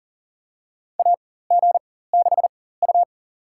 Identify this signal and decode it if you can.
Morse code, heard as a keyed tone, about 40 wpm, spaced as for 19 wpm, 700 Hz. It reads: AG6U
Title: USA Call Signs - 4 Characters - 4x Character Spacing - 40wpm